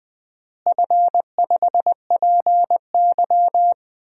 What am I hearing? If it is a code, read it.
Morse code, heard as a keyed tone, about 20 wpm, 700 Hz. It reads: F5PY
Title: DXCC-100 Call Signs - 4 Characters - Repeated 3x - 20wpm